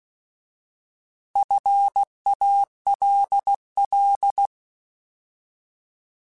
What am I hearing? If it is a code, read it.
Morse code, heard as a keyed tone, about 16 wpm, 800 Hz. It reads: FALL
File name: Alphabetical / Alphabetical mp3s / Fs @15